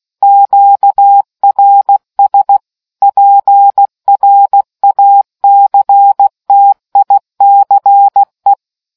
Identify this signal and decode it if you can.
Morse code, heard as a keyed tone, about 16 wpm, 800 Hz. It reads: QRS PRACTICE